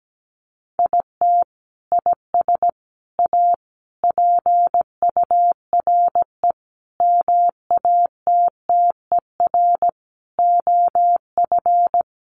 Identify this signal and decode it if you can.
Morse code, heard as a keyed tone, about 17 wpm, 700 Hz. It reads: IT IS A PURE MATTER OF
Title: Sentences from Top 1500 Words 17wpm